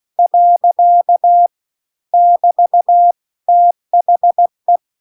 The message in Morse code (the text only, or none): . <BT> THE